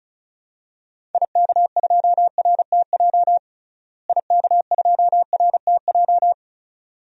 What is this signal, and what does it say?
Morse code, heard as a keyed tone, about 36 wpm, 700 Hz. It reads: IK2RTJ IK2RTJ
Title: DXCC-100 Call Signs - 6 Characters - Repeated 2x - 35wpm